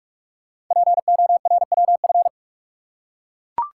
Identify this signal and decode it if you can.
Morse code, heard as a keyed tone, about 44 wpm, 700 Hz. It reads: PORWF
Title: Code Groups - 5 Letters - 45wpm